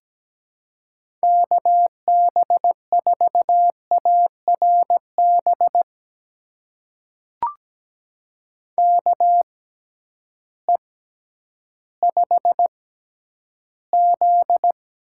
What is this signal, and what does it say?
Morse code, heard as a keyed tone, about 17 wpm, spaced as for 10 wpm, 700 Hz. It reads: KB4ARB KE5Z
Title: USA Call Signs - 6x Character Spacing - 17wpm